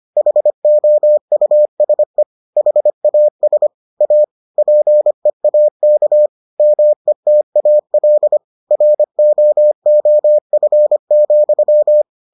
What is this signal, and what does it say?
Morse code, heard as a keyed tone, about 25 wpm, 600 Hz. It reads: HOUSE HAS A PEAK METAL ROOF,